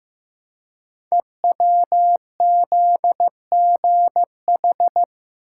Morse code, heard as a keyed tone, about 15 wpm, 700 Hz. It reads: EWZGH
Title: Code Groups - 5 Letters - 15wpm